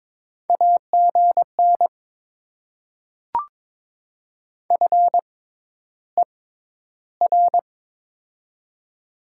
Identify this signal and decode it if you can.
Morse code, heard as a keyed tone, about 22 wpm, spaced as for 11 wpm, 700 Hz. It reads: AGN FER
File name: Top 20 QSO Elements - 6x Character Spacing - 22wpm